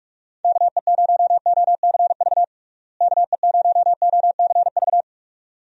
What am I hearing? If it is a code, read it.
Morse code, heard as a keyed tone, about 46 wpm, 700 Hz. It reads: KE0OCV KE0OCV
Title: USA Call Signs - 6 Characters - Repeated 2x - 45wpm